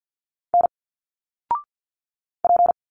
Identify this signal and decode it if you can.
Morse code, heard as a keyed tone, about 50 wpm, 700 Hz. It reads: N P
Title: Single Letters 50wpm